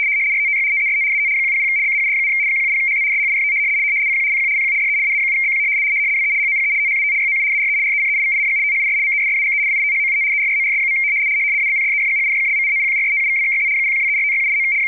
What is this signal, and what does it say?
RTTY, about 100 baud. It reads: RTTY TEST AT 100 BAUD, 170HZ SHIFT CQ CQ CQ DE AA7BQ WE HOPE THAT YOU ARE RECEIVING THIS MESSAGE WITHOUT ERRORS. ABCDEFGHIJKLMNOPQRSTUVWXYZ 0123456789
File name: rtty170 / rtty100